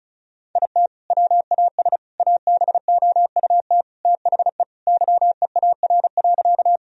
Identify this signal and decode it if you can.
Morse code, heard as a keyed tone, about 36 wpm, 700 Hz. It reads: IT WAS ABOUT THE YEAR.
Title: Sentences from Top 1000 Words - Review Part 1 - 35wpm